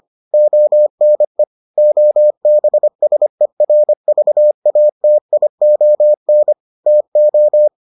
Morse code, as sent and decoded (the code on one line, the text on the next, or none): --- -. / --- -... ... .-. ...- .- - .. --- -. / - ---
ONE OBSERVATION TO